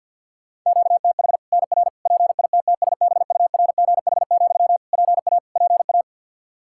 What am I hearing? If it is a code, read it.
Morse code, heard as a keyed tone, about 51 wpm, 700 Hz. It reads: QTH NR PITTSBURGH, PA PA
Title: Practice QSO - Exchange 1 - Beginning - 50wpm